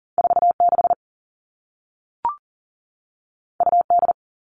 Morse code, heard as a keyed tone, about 40 wpm, 700 Hz. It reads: 46 UD
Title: All Cut Numbers - 2-Digit 40wpm